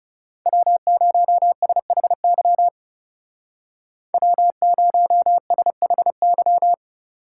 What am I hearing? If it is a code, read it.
Morse code, heard as a keyed tone, about 39 wpm, 700 Hz. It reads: W0SHY W0SHY